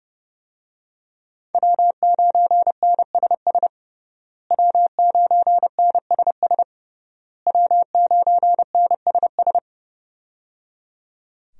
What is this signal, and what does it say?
Morse code, heard as a keyed tone, about 30 wpm, 700 Hz. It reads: W9NSS W9NSS W9NSS
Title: USA Call Signs - 5 Characters - Repeated 3x - 30wpm